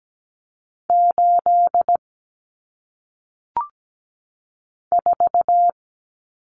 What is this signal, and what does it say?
Morse code, heard as a keyed tone, about 17 wpm, 700 Hz. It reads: 8 4
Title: Numbers - 1-Digit 17wpm